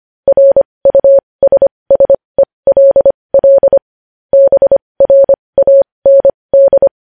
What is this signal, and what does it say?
Morse code, heard as a keyed tone, about 25 wpm, spaced as for 20 wpm, 550 Hz. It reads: RUSSELL BRAND
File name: podcast / News headlines-1x-25x20